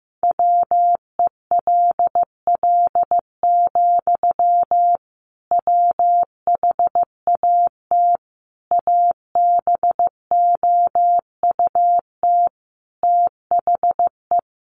Morse code, heard as a keyed tone, about 15 wpm, 700 Hz. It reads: WELL, WHAT ABOUT THE
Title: A Sound Of Thunder 15wpm